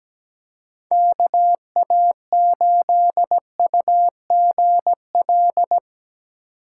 Morse code, heard as a keyed tone, about 18 wpm, 700 Hz. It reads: KA8UGL
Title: Speed Racing - USA 2x3 Call Signs 17wpm